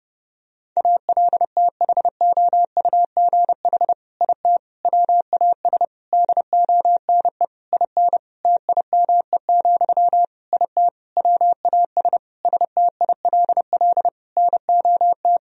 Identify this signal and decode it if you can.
Morse code, heard as a keyed tone, about 32 wpm, 700 Hz. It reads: ALTHOUGH IT WAS DONE IN TIME, IT WAS STILL NOT
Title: Sentences from Top 1000 Words - Review Part 3 - 30wpm